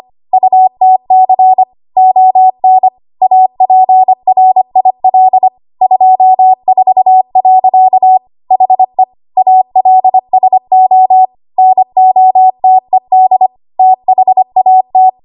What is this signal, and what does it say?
Morse code, heard as a keyed tone, about 25 wpm, 750 Hz. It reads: UTC ON APRIL 24. HE ALSO NOTED THAT
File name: Morse / 25WPM